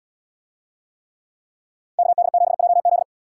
Morse code, heard as a keyed tone, about 68 wpm, 700 Hz. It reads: KD7JZ